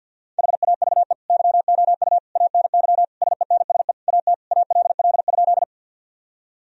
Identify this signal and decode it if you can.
Morse code, heard as a keyed tone, about 50 wpm, 700 Hz. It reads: HAVE YOU ANY SENSE AT ALL?
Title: Sentences from Top 1000 Words - Review Part 4 - 50wpm